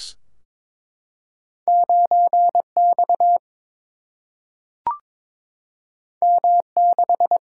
Morse code, heard as a keyed tone, about 21 wpm, 700 Hz. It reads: 9 X M 6
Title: Single-Letter Single-Number 22wpm